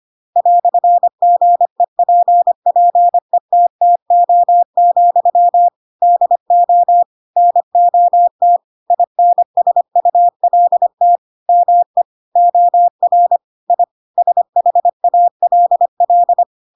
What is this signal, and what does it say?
Morse code, heard as a keyed tone, about 25 wpm, 700 Hz. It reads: "GEPPETTO, DO NOT INSULT ME OR I SHALL
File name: Pinocchio / Pinocchio ch2 25 25 ik1hge